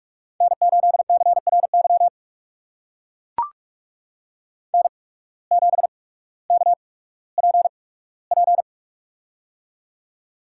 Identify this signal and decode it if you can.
Morse code, heard as a keyed tone, about 44 wpm, spaced as for 19 wpm, 700 Hz. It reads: N8CRY N7KPP
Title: USA Call Signs - 5 Characters - 8x Character Spacing - 45wpm